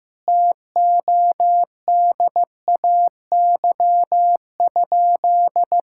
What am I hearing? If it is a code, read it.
Morse code, heard as a keyed tone, about 14 wpm, 700 Hz. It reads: TODAY?
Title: Sentences from Top 1600 Words 15wpm